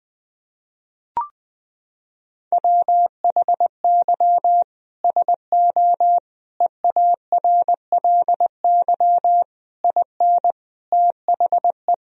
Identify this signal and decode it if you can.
Morse code, heard as a keyed tone, about 20 wpm, 700 Hz. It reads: WHY SO EARLY IN THE
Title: Sentences from Top 1000 Words - Review Part 2 - 20wpm